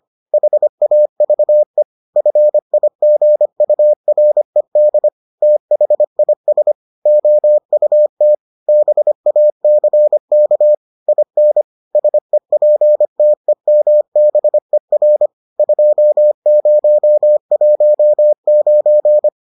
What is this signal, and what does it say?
Morse code, heard as a keyed tone, about 25 wpm, 600 Hz. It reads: HAVE FIGURED THIS OUT BACK IN SEPTEMBER 2019